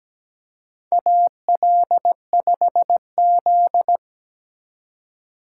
Morse code, heard as a keyed tone, about 17 wpm, 700 Hz. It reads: AL5Z
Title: USA Call Signs - 4x Character Spacing - 17wpm